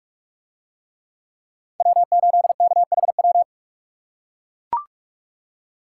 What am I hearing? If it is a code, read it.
Morse code, heard as a keyed tone, about 40 wpm, 700 Hz. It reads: W8KHW